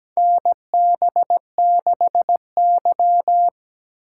Morse code, heard as a keyed tone, about 17 wpm, 700 Hz. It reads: NB6Y
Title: USA Call Signs - 4 Characters - 6x Character Spacing - 17wpm